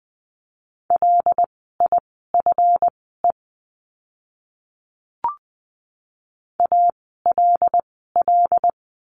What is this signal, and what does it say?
Morse code, heard as a keyed tone, about 20 wpm, spaced as for 15 wpm, 700 Hz. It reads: LIFE ALL
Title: 146 Lesson 24f - Letter y - Words 20wpm